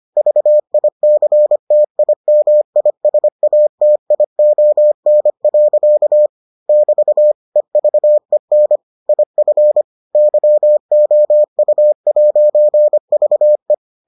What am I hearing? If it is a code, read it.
Morse code, heard as a keyed tone, about 25 wpm, 600 Hz. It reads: VICTIMISATION. <BT> EVEN IF YOU'VE